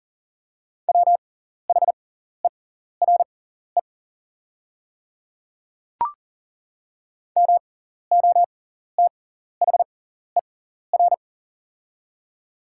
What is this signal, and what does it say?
Morse code, heard as a keyed tone, about 40 wpm, spaced as for 14 wpm, 700 Hz. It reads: WHERE MOTHER
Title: 6x Character Spacing - Top 200 Words - 40wpm